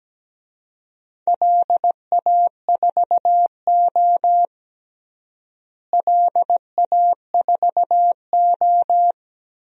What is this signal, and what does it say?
Morse code, heard as a keyed tone, about 17 wpm, 700 Hz. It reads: LA4O LA4O
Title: DXCC-100 Call Signs - 4 Characters - Repeated 3x - 17wpm